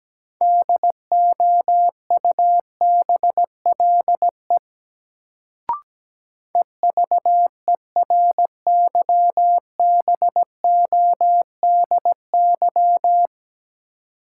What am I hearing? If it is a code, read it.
Morse code, heard as a keyed tone, about 17 wpm, 700 Hz. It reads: DOUBLE EVERYBODY
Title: Sentences from Top 1400 Words - New Word Drill - 17wpm